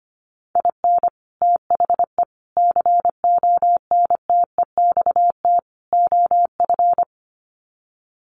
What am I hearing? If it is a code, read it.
Morse code, heard as a keyed tone, about 25 wpm, 700 Hz. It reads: IN THE CONTEXT OF
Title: Sets of 4 Words 25wpm